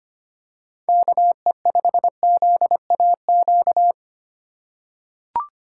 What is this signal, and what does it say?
Morse code, heard as a keyed tone, about 25 wpm, 700 Hz. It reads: KE5ZAQ
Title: USA Call Signs - 6 Characters - 4x Character Spacing - 25wpm